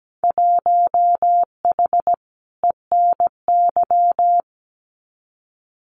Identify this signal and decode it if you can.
Morse code, heard as a keyed tone, about 17 wpm, 700 Hz. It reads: 1H ENY
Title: Winter Field Day Exchanges 17wpm